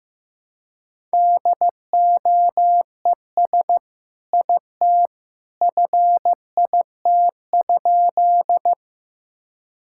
Morse code, heard as a keyed tone, about 15 wpm, 700 Hz. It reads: DOES IT FIT?
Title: Sentences from Top 1000 Words - Review Part 5 - 15wpm